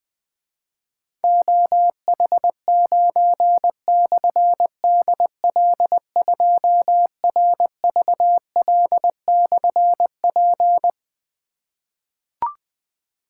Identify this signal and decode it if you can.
Morse code, heard as a keyed tone, about 20 wpm, 700 Hz. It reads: OH9/DL2RVL/P
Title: Difficult Call Signs 20wpm